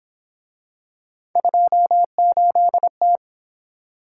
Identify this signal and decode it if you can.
Morse code, heard as a keyed tone, about 27 wpm, 700 Hz. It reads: 28T
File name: Speed-Racing - Three Letter-Number Combinations - 22wpm